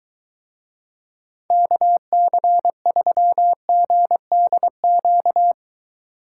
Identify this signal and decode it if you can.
Morse code, heard as a keyed tone, about 23 wpm, 700 Hz. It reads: KC3GDQ